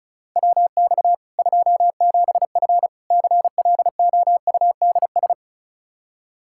Morse code, heard as a keyed tone, about 34 wpm, 700 Hz. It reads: WX 27F CLOUDS
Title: K1USN SST QSO Practice 35wpm